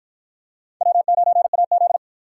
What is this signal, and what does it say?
Morse code, heard as a keyed tone, about 49 wpm, 700 Hz. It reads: W9AZ